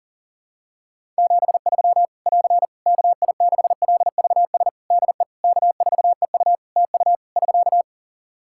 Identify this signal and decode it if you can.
Morse code, heard as a keyed tone, about 37 wpm, 700 Hz. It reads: 73 <AR> KI6LVS DE K4EU TU <SK>
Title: K1USN SST QSO Practice 40wpm